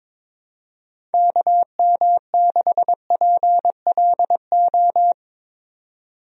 Morse code, heard as a keyed tone, about 22 wpm, 700 Hz. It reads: KM6PLO
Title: USA Call Signs - 6 Characters - 22wpm